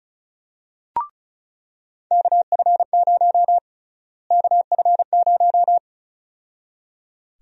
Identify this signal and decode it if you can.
Morse code, heard as a keyed tone, about 36 wpm, 700 Hz. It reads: KF0 KF0
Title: USA Call Sign Prefixes - Repeated 2x - 35wpm